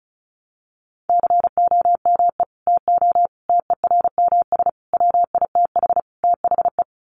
Morse code, heard as a keyed tone, about 34 wpm, 700 Hz. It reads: COME TO TERMS WITH THE
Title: Sets of 5 Words Encore2 35wpm